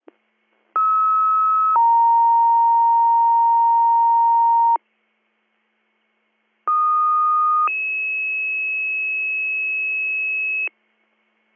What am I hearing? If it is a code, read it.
Two-tone page, heard as a one-second tone, then a three-second tone, about 1290 then 930 Hz, about 1230 then 2600 Hz. It is heard from 0.8 to 10.7 s.